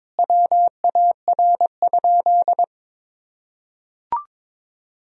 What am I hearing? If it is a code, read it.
Morse code, heard as a keyed tone, about 22 wpm, 700 Hz. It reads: WAR?
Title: Sentences from Top 1000 Words - Review Part 2 - 22wpm